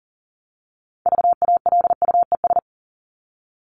Morse code, heard as a keyed tone, about 40 wpm, 700 Hz. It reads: VALUES
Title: ES:S Suffix - Top 100 Words with Base Words 40wpm